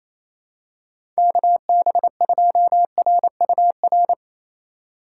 Morse code, heard as a keyed tone, about 28 wpm, 700 Hz. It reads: KB2RUR
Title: USA Call Signs - 6 Characters - 4x Character Spacing - 28wpm